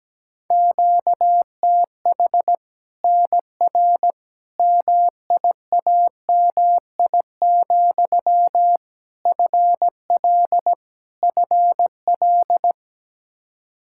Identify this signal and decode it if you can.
Morse code, heard as a keyed tone, about 17 wpm, 700 Hz. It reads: QTH NR MIAMI, FL FL
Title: Practice QSO - Exchange 1 - Beginning - 17wpm